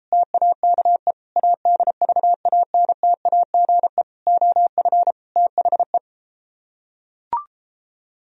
Morse code, heard as a keyed tone, about 33 wpm, 700 Hz. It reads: TAKE ADVANTAGE OF THE